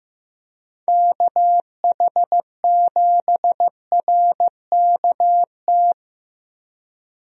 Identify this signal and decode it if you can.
Morse code, heard as a keyed tone, about 15 wpm, 700 Hz. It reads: KH7RKT
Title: USA Call Signs - 6 Characters - Repeated 3x - 15wpm